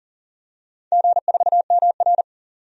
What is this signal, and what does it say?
Morse code, heard as a keyed tone, about 42 wpm, 700 Hz. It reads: G4MR
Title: DXCC-100 Call Signs - 4 Characters - 40wpm